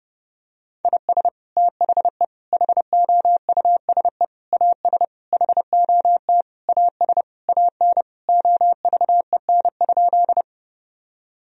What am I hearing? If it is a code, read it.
Morse code, heard as a keyed tone, about 30 wpm, 700 Hz. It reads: IS THE HOUSE AS HOT AS AN OVEN?
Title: Sentences from Top 1000 Words - Review Part 4 - 30wpm